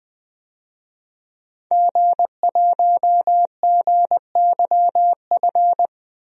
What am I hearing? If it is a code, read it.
Morse code, heard as a keyed tone, about 20 wpm, 700 Hz. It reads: G1GYF
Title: DXCC-100 Call Signs - 5 Characters - 20wpm